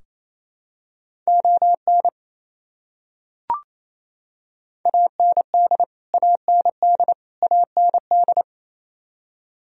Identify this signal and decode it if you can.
Morse code, heard as a keyed tone, about 28 wpm, 700 Hz. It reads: ON AND AND AND
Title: Top 100 Words Repeated 28wpm